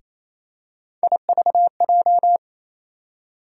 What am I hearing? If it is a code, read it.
Morse code, heard as a keyed tone, about 28 wpm, 700 Hz. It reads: IVJ
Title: USA Call Sign Postfixes - 3 Letters 28wpm